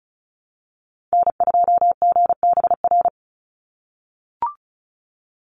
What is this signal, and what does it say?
Morse code, heard as a keyed tone, about 34 wpm, 700 Hz. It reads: N2GBR
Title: NA SOTA Activator Callsigns 35wpm